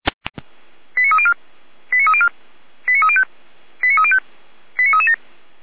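Five-tone selective call (ZVEI), heard as quick runs of five tones, tones of about 70 ms. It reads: 89304 89304 89305 89305 89337
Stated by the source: ZVEI five-tone selective call 89270 received by SDR.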